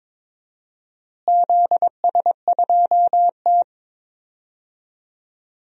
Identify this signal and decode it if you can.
Morse code, heard as a keyed tone, about 22 wpm, 700 Hz. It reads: ZS2T
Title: DXCC-100 Call Signs - 4 Characters - Repeated 3x - 22wpm